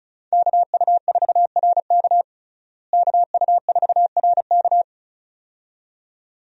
Morse code, heard as a keyed tone, about 36 wpm, 700 Hz. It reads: KU4RK KU4RK
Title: USA Call Signs - 5 Characters - Repeated 2x - 35wpm